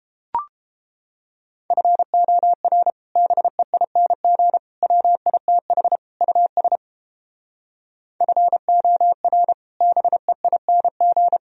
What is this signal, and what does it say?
Morse code, heard as a keyed tone, about 29 wpm, 700 Hz. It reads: FOR BEING WITH US FOR BEING